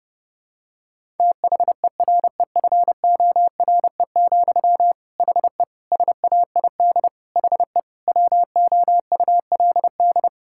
Morse code, heard as a keyed tone, about 32 wpm, 700 Hz. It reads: THEREFORE, HE SAID HE WOULD
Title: Sentences from Top 1000 Words - Review Part 5 - 30wpm